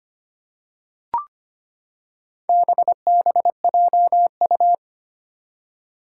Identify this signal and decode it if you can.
Morse code, heard as a keyed tone, about 25 wpm, 700 Hz. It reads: BBJU
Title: Four Challenging Letter-Number Combinations 25wpm